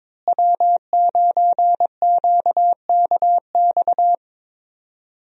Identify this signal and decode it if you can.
Morse code, heard as a keyed tone, about 22 wpm, 700 Hz. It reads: W9QKX
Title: USA Call Signs - 5 Characters - Repeated 2x - 22wpm